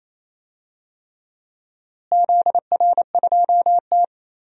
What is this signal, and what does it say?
Morse code, heard as a keyed tone, about 28 wpm, 700 Hz. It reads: ZR2T
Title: DXCC-100 Call Signs - 4 Characters - Repeated 3x - 28wpm